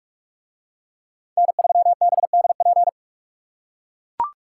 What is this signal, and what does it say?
Morse code, heard as a keyed tone, about 43 wpm, 700 Hz. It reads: N3BDP